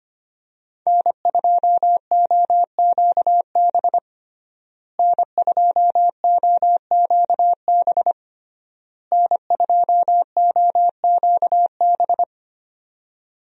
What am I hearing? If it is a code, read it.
Morse code, heard as a keyed tone, about 25 wpm, 700 Hz. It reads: N2OQB N2OQB N2OQB
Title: USA Call Signs - 5 Characters - Repeated 3x - 25wpm